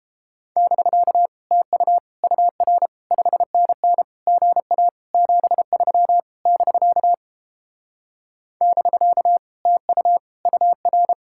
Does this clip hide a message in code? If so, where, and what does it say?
Yes: Morse code from 0.6 to 11.1 s, heard as a keyed tone, about 29 wpm, 700 Hz. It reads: <BK> TU UR 5NN GA 73 <BK> <BK> TU UR